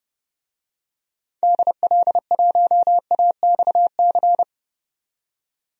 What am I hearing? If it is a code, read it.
Morse code, heard as a keyed tone, about 30 wpm, 700 Hz. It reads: DL1AXC
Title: DXCC-100 Call Signs - 6 Characters - 30wpm